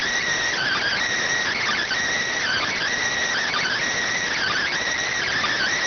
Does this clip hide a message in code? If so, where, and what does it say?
Yes: Emergency Alert System (SAME) tones (header bursts) from 0.5 to 4.8 s, heard as harsh buzzing bursts.